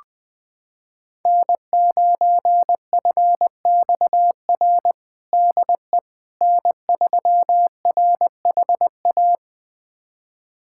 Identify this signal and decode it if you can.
Morse code, heard as a keyed tone, about 20 wpm, 700 Hz. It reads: N9FXR DE N3RHA
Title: USA Callsign DE Callsign - 5 Characters - 20wpm